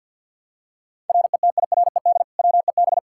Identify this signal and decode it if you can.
Morse code, heard as a keyed tone, about 46 wpm, 700 Hz. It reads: RETIRED WEB
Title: Practice QSO - Exchange 3 - More Info - 50wpm